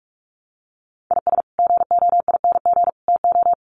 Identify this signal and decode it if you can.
Morse code, heard as a keyed tone, about 44 wpm, 700 Hz. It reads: IS GOING TO
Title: Sets of 3 Words 45wpm